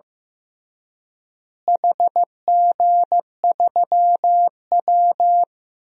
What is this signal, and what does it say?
Morse code, heard as a keyed tone, about 15 wpm, 700 Hz. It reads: HG3W
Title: DXCC-100 Call Signs - 4 Characters - Repeated 3x - 15wpm